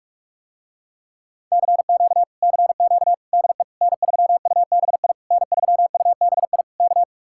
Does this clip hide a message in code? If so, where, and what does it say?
Yes: Morse code from 1.5 to 7.0 s, heard as a keyed tone, about 50 wpm, 700 Hz. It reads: CQ CQ DE N3UBI N3UBI K